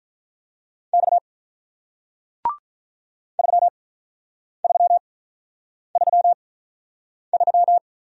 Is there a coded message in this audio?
Yes: Morse code from 0.9 to 7.8 s, heard as a keyed tone, about 48 wpm, spaced as for 9 wpm, 700 Hz. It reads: X 3333